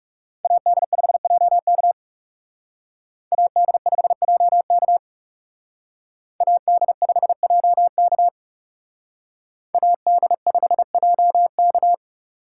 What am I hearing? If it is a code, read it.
Morse code, heard as a keyed tone, about 38 wpm, 700 Hz. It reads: AD5JK AD5JK AD5JK AD5JK